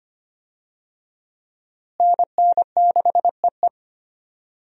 Morse code, heard as a keyed tone, about 25 wpm, 700 Hz. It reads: NN6EE
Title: USA Call Signs - 5 Characters - Repeated 2x - 25wpm